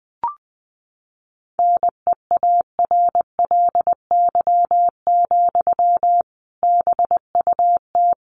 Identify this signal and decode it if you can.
Morse code, heard as a keyed tone, about 20 wpm, 700 Hz. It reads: NEARLY, BUT
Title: Sentences from Top 1300 Words 20wpm